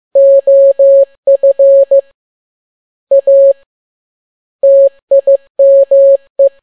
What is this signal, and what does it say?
Morse code, heard as a keyed tone, about 15 wpm, 550 Hz. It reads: OF A TIME
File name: n6tt p1 15C13